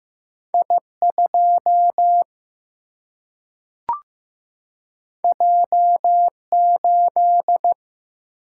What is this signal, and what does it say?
Morse code, heard as a keyed tone, about 15 wpm, 700 Hz. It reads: I 2 J 8
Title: Single-Letter Single-Number 15wpm